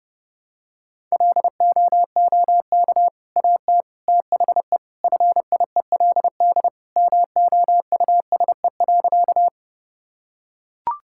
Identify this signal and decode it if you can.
Morse code, heard as a keyed tone, about 32 wpm, 700 Hz. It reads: LOOK AT THE FIELD MOUSE.
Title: Sentences from Top 1000 Words - Review Part 5 - 30wpm